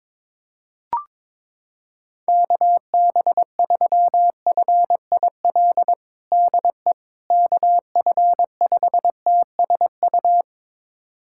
Morse code, heard as a keyed tone, about 22 wpm, 700 Hz. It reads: KB3FIL DE KF5TSU
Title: USA Callsign DE Callsign - 6 Characters - 22wpm